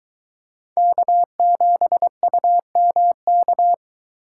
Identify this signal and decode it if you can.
Morse code, heard as a keyed tone, about 22 wpm, 700 Hz. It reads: K7UMK